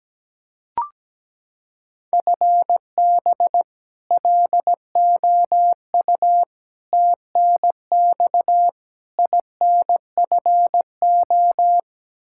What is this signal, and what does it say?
Morse code, heard as a keyed tone, about 17 wpm, 700 Hz. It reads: FB LOU TNX INFO
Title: K1USN SST QSO Practice 17wpm